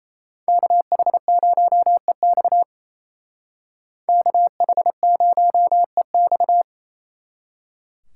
Morse code, heard as a keyed tone, about 32 wpm, 700 Hz. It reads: KH0EX KH0EX
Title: Speed Racing - USA 2x2 Call Signs 28wpm